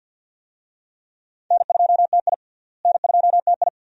Morse code, heard as a keyed tone, about 51 wpm, 700 Hz. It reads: N2TI N2TI
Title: USA Call Signs - 4 Characters - Repeated 2x - 50wpm